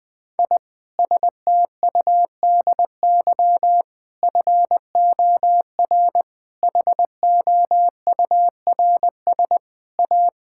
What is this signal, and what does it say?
Morse code, heard as a keyed tone, about 20 wpm, 700 Hz. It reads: I STUDY FOR HOURS A
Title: Sentences with 7-Word Phrases - 20wpm